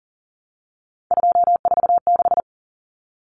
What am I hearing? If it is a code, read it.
Morse code, heard as a keyed tone, about 40 wpm, 700 Hz. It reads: 246
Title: Numbers - 3-Digit 40wpm